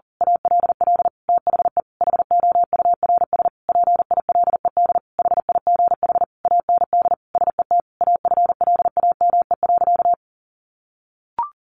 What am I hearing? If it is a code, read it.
Morse code, heard as a keyed tone, about 40 wpm, 700 Hz. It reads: ALL THE HOURS PILED HIGH AND SET AFLAME.
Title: A Sound Of Thunder 40wpm